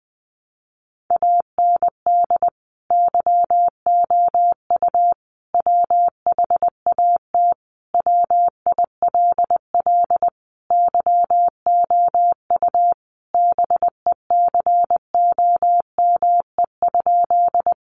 Morse code, heard as a keyed tone, about 20 wpm, 700 Hz. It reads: AND YOU WHAT WILL YOU BECOME?
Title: Sentences from Top 600 Words 20wpm